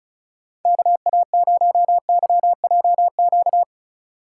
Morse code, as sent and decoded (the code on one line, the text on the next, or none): -.- .- ----- -.-- .--- --.-
KA0YJQ